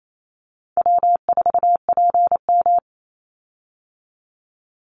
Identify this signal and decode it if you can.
Morse code, heard as a keyed tone, about 28 wpm, 700 Hz. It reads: W4PM
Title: Easy Call Signs Repeated 2x - 28wpm